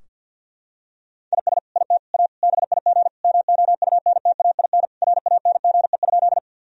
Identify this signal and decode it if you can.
Morse code, heard as a keyed tone, about 50 wpm, 700 Hz. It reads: IS IT A BIG MOUNTAIN RANGE?
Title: Sentences from Top 1000 Words - Review Part 4 - 50wpm